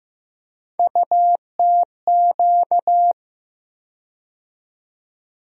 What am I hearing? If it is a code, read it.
Morse code, heard as a keyed tone, about 15 wpm, 700 Hz. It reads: UTQ
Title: USA Call Sign Postfixes - 3 Letters 15wpm